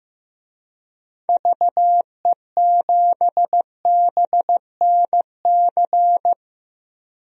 Morse code, heard as a keyed tone, about 16 wpm, 700 Hz. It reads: VE7BNC
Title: DXCC-100 Call Signs - 6 Characters - Repeated 2x - 15wpm